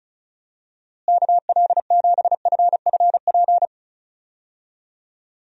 Morse code, heard as a keyed tone, about 34 wpm, 700 Hz. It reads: KL7FFP
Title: USA Call Signs - 6 Characters - Repeated 3x - 35wpm